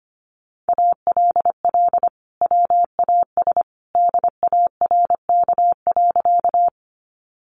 Morse code, heard as a keyed tone, about 25 wpm, 700 Hz. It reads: ALL WAS DARK.
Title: Sentences from Top 500 Words 25wpm